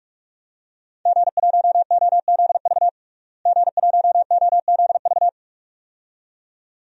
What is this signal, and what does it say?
Morse code, heard as a keyed tone, about 42 wpm, 700 Hz. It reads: G1OZV G1OZV
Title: DXCC-100 Call Signs - 5 Characters - Repeated 2x - 45wpm